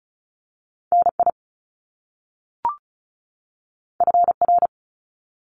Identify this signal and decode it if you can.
Morse code, heard as a keyed tone, about 36 wpm, 700 Hz. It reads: NI FR
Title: Top Two Letter Combinations in Words 35wpm